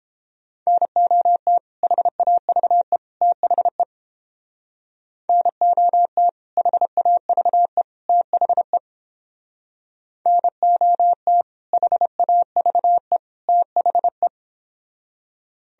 Morse code, heard as a keyed tone, about 31 wpm, 700 Hz. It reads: NOT HAVE THE NOT HAVE THE NOT HAVE THE